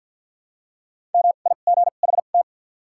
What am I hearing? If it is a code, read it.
Morse code, heard as a keyed tone, about 47 wpm, spaced as for 35 wpm, 700 Hz. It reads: MIGHT
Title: 2x Character Spacing - Top 500 Words - 50wpm